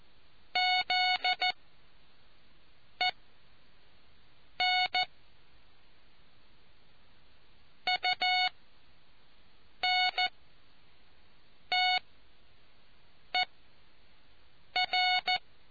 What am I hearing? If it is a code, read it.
Morse code, heard as a keyed tone, about 14 wpm, spaced as for 5 wpm, 730 Hz. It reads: ZEN UNTER